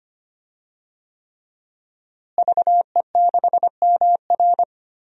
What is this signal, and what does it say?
Morse code, heard as a keyed tone, about 25 wpm, 700 Hz. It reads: VE6MR
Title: DXCC-100 Call Signs - 5 Characters - Repeated 3x - 25wpm